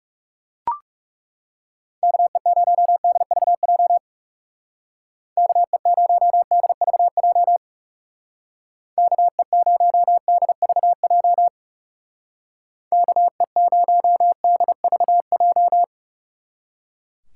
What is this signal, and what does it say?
Morse code, heard as a keyed tone, about 36 wpm, 700 Hz. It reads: KE0DVJ KE0DVJ KE0DVJ KE0DVJ
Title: Speed Racing - USA 2x3 Call Signs 30wpm